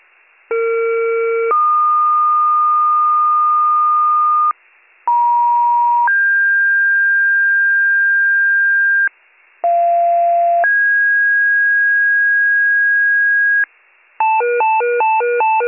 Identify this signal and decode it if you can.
Two-tone page, heard as a one-second tone, then a three-second tone, about 460 then 1220 Hz, about 950 then 1600 Hz, about 690 then 1650 Hz, with noise. It is heard from 0.5 to 13.6 s.